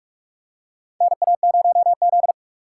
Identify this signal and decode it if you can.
Morse code, heard as a keyed tone, about 47 wpm, 700 Hz. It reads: NA0Z
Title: USA Call Signs - 4 Characters - 2x Character Spacing - 45wpm